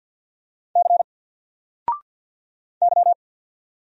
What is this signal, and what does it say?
Morse code, heard as a keyed tone, about 50 wpm, 700 Hz. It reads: C Y
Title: Single Letter-Number 50wpm